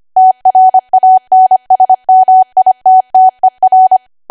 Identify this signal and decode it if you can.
Morse code, heard as a keyed tone, about 25 wpm, 750 Hz. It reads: TRANSMITTER
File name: Archive / 25 WPM / 25WPM